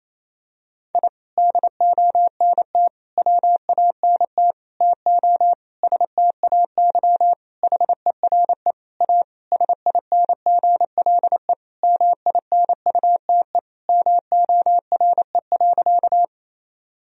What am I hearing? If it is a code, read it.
Morse code, heard as a keyed tone, about 28 wpm, 700 Hz. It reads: I DONT WANT TO STAY HERE A SINGLE MINUTE MORE.
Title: Sentences from Top 1000 Words - Review Part 4 - 28wpm